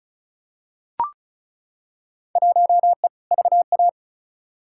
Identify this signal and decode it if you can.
Morse code, heard as a keyed tone, about 34 wpm, 700 Hz. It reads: 1E VA
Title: Field Day Exchanges - 35wpm